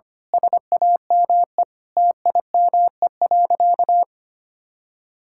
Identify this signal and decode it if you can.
Morse code, heard as a keyed tone, about 25 wpm, 700 Hz. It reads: SAME TIME.
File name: practice / 25wpm / Difficult Sentences with Hints - 25wpm